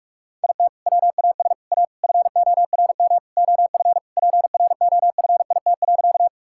Morse code, heard as a keyed tone, about 46 wpm, 700 Hz. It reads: IT WAS A FORM OF PROFIT.
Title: Sentences from Top 1000 Words - Review Part 5 - 45wpm